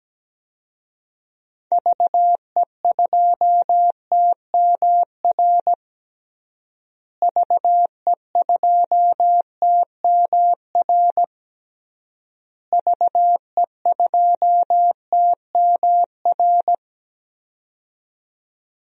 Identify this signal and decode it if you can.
Morse code, heard as a keyed tone, about 17 wpm, 700 Hz. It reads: VE2TMR VE2TMR VE2TMR
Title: DXCC-100 Call Signs - 6 Characters - Repeated 3x - 17wpm